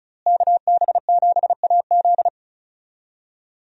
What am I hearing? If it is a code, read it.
Morse code, heard as a keyed tone, about 36 wpm, 700 Hz. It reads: KB7AZ
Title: USA Call Signs - 5 Characters - Repeated 3x - 35wpm